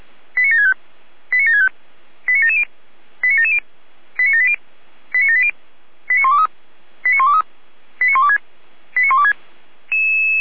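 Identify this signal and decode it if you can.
Five-tone selective call (ZVEI), heard as quick runs of five tones, tones of about 80 ms. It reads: 89765 89765 89770 89770 89780 89780 89123 89123 89126 89126
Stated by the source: ZVEI five-tone selective call received by SDR.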